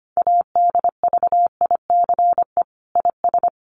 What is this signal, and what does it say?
Morse code, heard as a keyed tone, about 25 wpm, 700 Hz. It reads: ADVICE IS
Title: Sentences from Top 1100 Words 25wpm